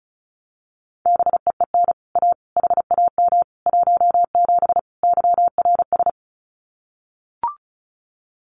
Morse code, heard as a keyed tone, about 34 wpm, 700 Hz. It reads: BEEN A HAM 17 YRS
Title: Common QSO Phrases Encore 35wpm